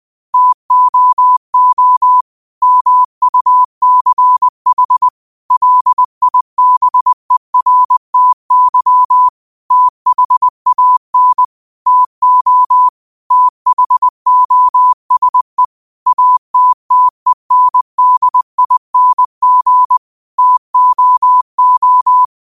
Morse code, heard as a keyed tone, about 20 wpm, 1000 Hz. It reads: TOO MUCH LIBERTY THAN TO THOSE ATTENDING TOO